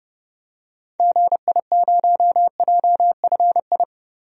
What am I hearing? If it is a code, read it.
Morse code, heard as a keyed tone, about 30 wpm, 700 Hz. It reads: GI0JFI
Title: DXCC-100 Call Signs - 6 Characters - Repeated 3x - 30wpm